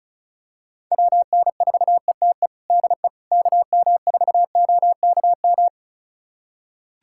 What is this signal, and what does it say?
Morse code, heard as a keyed tone, about 36 wpm, 700 Hz. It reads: WN4ETE DE KM4OKM
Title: USA Callsign DE Callsign 35wpm